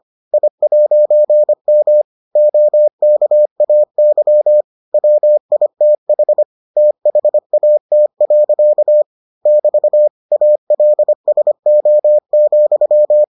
Morse code, heard as a keyed tone, about 25 wpm, 600 Hz. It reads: I'M OKAY WITH THAT. <BT> ALSO,